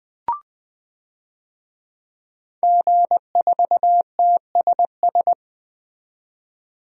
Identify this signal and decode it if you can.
Morse code, heard as a keyed tone, about 20 wpm, 700 Hz. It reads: G4TSS
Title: DXCC-100 Call Signs - 5 Characters - 20wpm